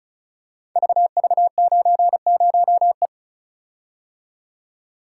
Morse code, heard as a keyed tone, about 36 wpm, 690 Hz. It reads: VV90E
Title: Code Groups - 5 Letter:Numbers - 35wpm